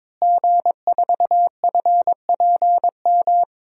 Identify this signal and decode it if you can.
Morse code, heard as a keyed tone, about 22 wpm, 700 Hz. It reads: G4FPM
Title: DXCC-100 Call Signs - 5 Characters - Repeated 2x - 22wpm